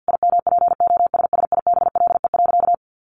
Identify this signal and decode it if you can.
Morse code, heard as a keyed tone, about 48 wpm, 700 Hz. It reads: IMPOSSIBLE.
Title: Sentences from Top 1100 Words 50wpm